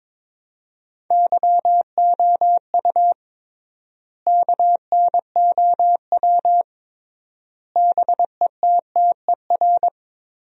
Morse code, heard as a keyed tone, about 22 wpm, 700 Hz. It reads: YOU KNOW BETTER
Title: 3x Word-Spacing - Sentences from Top 400 Words - 22wpm